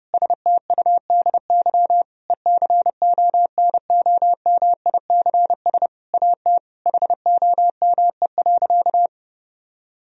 Morse code, heard as a keyed tone, about 30 wpm, 700 Hz. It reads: STUDY ECONOMICS AT HOME.
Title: Sentences from Top 1000 Words - Review Part 3 - 30wpm